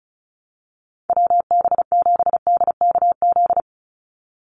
Morse code, heard as a keyed tone, about 34 wpm, 700 Hz. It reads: WB7DKZ
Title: Call Signs Repeated 3x - 35wpm